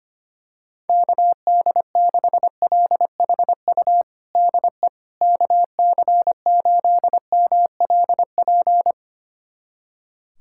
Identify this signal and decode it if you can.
Morse code, heard as a keyed tone, about 25 wpm, 700 Hz. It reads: KD6LHU DE KC8MLP
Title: USA Callsign DE Callsign 25wpm